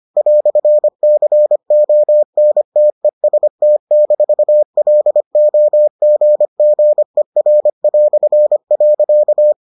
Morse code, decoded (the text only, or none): "CONTEST-LOGGER".